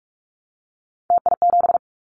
Morse code, heard as a keyed tone, about 46 wpm, 700 Hz. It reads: TI7
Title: Callsign Prefixes for Top 100 DXCC - 45wpm